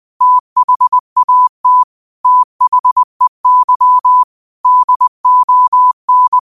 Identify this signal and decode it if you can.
Morse code, heard as a keyed tone, about 20 wpm, 1000 Hz. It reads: THAT THEY DON
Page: New quotes every day in morse code at 20 Words per minute.